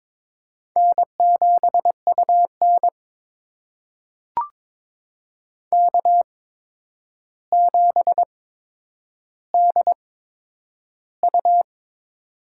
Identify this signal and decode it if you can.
Morse code, heard as a keyed tone, about 22 wpm, spaced as for 9 wpm, 700 Hz. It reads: N7UN K7DU
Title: USA Call Signs - 4 Characters - 8x Character Spacing - 22wpm